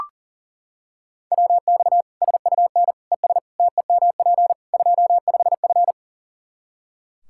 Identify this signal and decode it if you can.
Morse code, heard as a keyed tone, about 37 wpm, 700 Hz. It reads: WX SUN ES TEMP 25F
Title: Practice QSO - Exchange 2 - Info - 40wpm